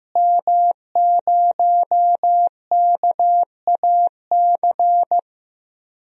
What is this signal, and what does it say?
Morse code, heard as a keyed tone, about 15 wpm, 700 Hz. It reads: M0KAC
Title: DXCC-100 Call Signs - 5 Characters - Repeated 3x - 15wpm